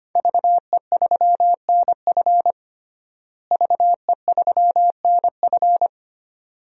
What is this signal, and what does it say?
Morse code, heard as a keyed tone, about 25 wpm, 700 Hz. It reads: VE3NF VE3NF
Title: DXCC-100 Call Signs - 5 Characters - Repeated 2x - 25wpm